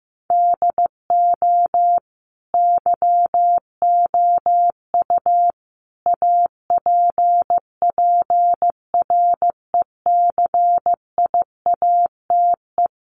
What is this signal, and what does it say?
Morse code, heard as a keyed tone, about 15 wpm, 700 Hz. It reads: DO YOU APPRECIATE
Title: Sentences from Top 1300 Words 15wpm